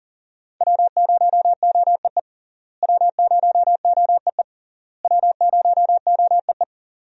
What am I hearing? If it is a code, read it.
Morse code, heard as a keyed tone, about 41 wpm, 700 Hz. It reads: W0OEE W0OEE W0OEE
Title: USA Call Signs - 5 Characters - Repeated 3x - 40wpm